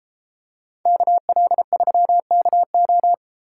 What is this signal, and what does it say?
Morse code, heard as a keyed tone, about 32 wpm, 700 Hz. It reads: KL3KO